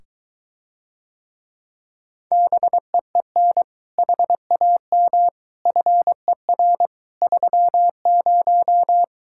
Morse code, heard as a keyed tone, about 23 wpm, 700 Hz. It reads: BEEN HAM FER 30